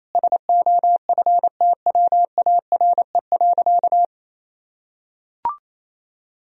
Morse code, heard as a keyed tone, about 27 wpm, 700 Hz. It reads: SOFTWARE.
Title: Sentences from Top 1000 Words - Review Part 3 - 28wpm